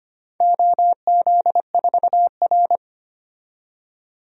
Morse code, heard as a keyed tone, about 25 wpm, 700 Hz. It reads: OZ4R
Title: DXCC-100 Call Signs - 4 Characters - 25wpm